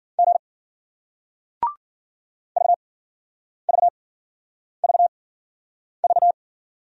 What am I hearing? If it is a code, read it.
Morse code, heard as a keyed tone, about 56 wpm, spaced as for 9 wpm, 700 Hz. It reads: G VVVV